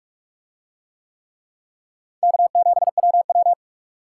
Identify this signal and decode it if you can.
Morse code, heard as a keyed tone, about 44 wpm, 700 Hz. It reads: K7WW